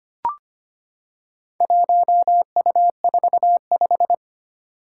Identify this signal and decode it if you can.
Morse code, heard as a keyed tone, about 25 wpm, 700 Hz. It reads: 1U45
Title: Four Challenging Letter-Number Combinations 25wpm